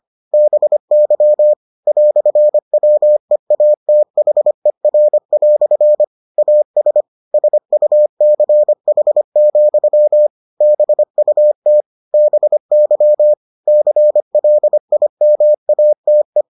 Morse code, heard as a keyed tone, about 25 wpm, 600 Hz. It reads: BY "WEATHER" AS SUCH, BUT BY CLIMATE